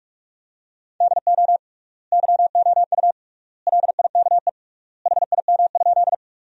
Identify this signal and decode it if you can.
Morse code, heard as a keyed tone, about 47 wpm, 700 Hz. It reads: DO YOU LIKE HIM?
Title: 3x Word-Spacing - Sentences from Top 100 Words - 45wpm